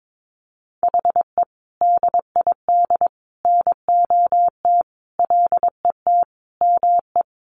Morse code, heard as a keyed tone, about 22 wpm, 700 Hz. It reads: HE DID NOT LET ME
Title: Sentences from Top 1200 Words 22wpm